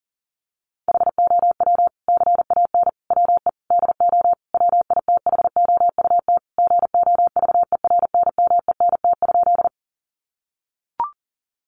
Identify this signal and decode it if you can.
Morse code, heard as a keyed tone, about 40 wpm, 700 Hz. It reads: HOW CAN WE DO WITHOUT GOVERNMENT?
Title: Sentences from Top 600 Words 40wpm